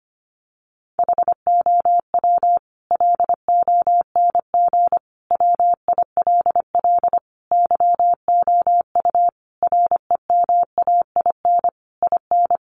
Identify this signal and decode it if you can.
Morse code, heard as a keyed tone, about 25 wpm, 700 Hz. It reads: HOW LONG WILL YOU REMAIN IN
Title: Sentences from Top 1200 Words 25wpm